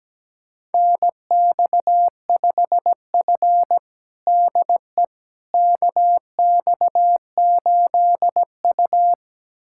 Morse code, heard as a keyed tone, about 17 wpm, 700 Hz. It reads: NX5F DE KX8U
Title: USA Callsign DE Callsign - 4 Characters - 17wpm